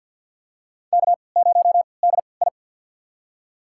Morse code, heard as a keyed tone, about 53 wpm, spaced as for 29 wpm, 700 Hz. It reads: K0DI
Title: USA Call Signs - 4 Characters - 3x Character Spacing - 50wpm